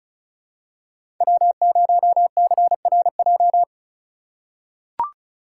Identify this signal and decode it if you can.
Morse code, heard as a keyed tone, about 36 wpm, 700 Hz. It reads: W0CRJ
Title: USA Call Signs - 5 Characters - 4x Character Spacing - 35wpm